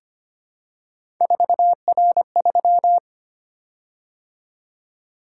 Morse code, heard as a keyed tone, about 25 wpm, 700 Hz. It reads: 4R3
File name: Three Letter-Number Combinations - 25wpm